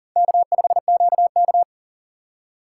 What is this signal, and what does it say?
Morse code, heard as a keyed tone, about 40 wpm, 700 Hz. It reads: K5QK
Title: USA Call Signs - 3x Character Spacing - 40wpm